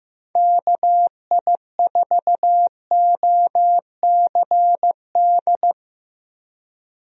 Morse code, heard as a keyed tone, about 15 wpm, 700 Hz. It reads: KI4OCD
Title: USA Call Signs - 6 Characters - 15wpm